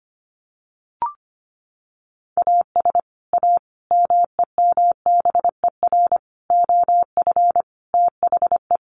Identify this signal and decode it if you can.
Morse code, heard as a keyed tone, about 25 wpm, 700 Hz. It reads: AS A MEMBER OF THE
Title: Sets of 5 Words 25wpm